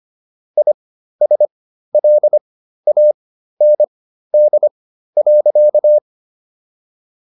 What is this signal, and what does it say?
Morse code, heard as a keyed tone, about 25 wpm, spaced as for 13 wpm, 600 Hz. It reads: ISLAND.